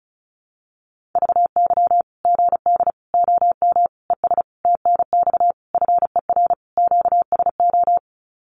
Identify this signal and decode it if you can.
Morse code, heard as a keyed tone, about 34 wpm, 700 Hz. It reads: VY GD OM ES TNX FER QSO
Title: Common QSO Phrases 35wpm